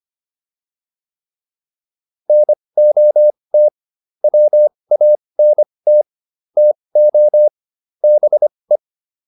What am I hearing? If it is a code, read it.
Morse code, heard as a keyed tone, about 25 wpm, spaced as for 20 wpm, 600 Hz. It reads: NOT WANT TO BE
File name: WD304 20